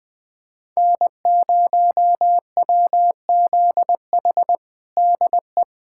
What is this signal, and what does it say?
Morse code, heard as a keyed tone, about 21 wpm, 700 Hz. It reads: N0WZH DE
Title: USA Callsign DE Callsign - 5 Characters - 20wpm